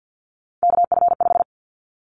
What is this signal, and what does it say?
Morse code, heard as a keyed tone, about 50 wpm, 700 Hz. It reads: KF5
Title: Callsign Prefixes for Top 25 DXCC - 50wpm